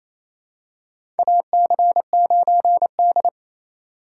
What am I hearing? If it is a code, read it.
Morse code, heard as a keyed tone, about 28 wpm, 700 Hz. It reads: AC9D
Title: USA Call Signs - 4 Characters - 4x Character Spacing - 28wpm